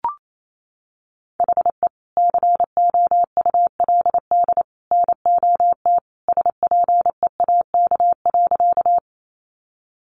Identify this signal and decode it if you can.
Morse code, heard as a keyed tone, about 28 wpm, 700 Hz. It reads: HE COULD NOT SPEAK.
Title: Sentences from Top 1100 Words 28wpm